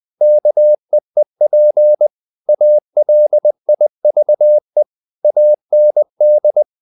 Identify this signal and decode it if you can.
Morse code, heard as a keyed tone, about 20 wpm, 600 Hz. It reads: KEEP ALIVE AND
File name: the old man and the sea part 2 20wpm